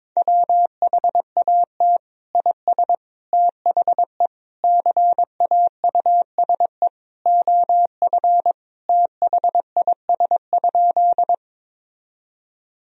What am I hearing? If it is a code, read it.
Morse code, heard as a keyed tone, about 21 wpm, 700 Hz. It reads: WHAT IS THE CAUSE OF THIS?
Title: Sentences from Top 1000 Words - Review Part 1 - 22wpm